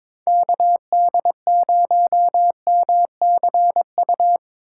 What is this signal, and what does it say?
Morse code, heard as a keyed tone, about 22 wpm, 700 Hz. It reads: KD0MCU
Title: USA Call Signs - 6 Characters - 4x Character Spacing - 22wpm